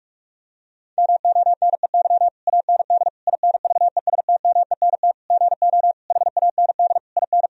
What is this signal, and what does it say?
Morse code, heard as a keyed tone, about 46 wpm, 700 Hz. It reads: MONEY AND INVESTMENT GO HAND IN
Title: Sentences from Top 1000 Words - Review Part 4 - 45wpm